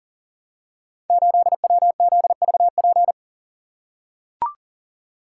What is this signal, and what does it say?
Morse code, heard as a keyed tone, about 42 wpm, 700 Hz. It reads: 8WZVP
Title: Code Groups - 5 Letter:Numbers - 40wpm